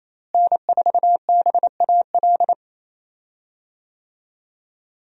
Morse code, heard as a keyed tone, about 28 wpm, 700 Hz. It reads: N4BAL
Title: USA Call Signs - 5 Characters - Repeated 3x - 28wpm